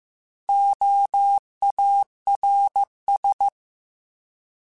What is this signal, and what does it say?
Morse code, heard as a keyed tone, about 15 wpm, 800 Hz. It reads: OARS